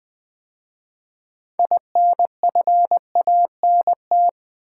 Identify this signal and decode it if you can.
Morse code, heard as a keyed tone, about 20 wpm, 700 Hz. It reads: INFANT
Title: 6 Letter Words - First 3 Letters Given - 20wpm